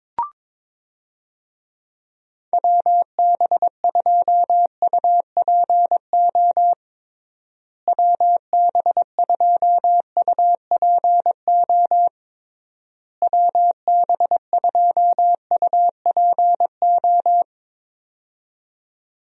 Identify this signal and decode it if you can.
Morse code, heard as a keyed tone, about 22 wpm, 700 Hz. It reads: WB2UPO WB2UPO WB2UPO
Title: USA Call Signs - 6 Characters - Repeated 3x - 22wpm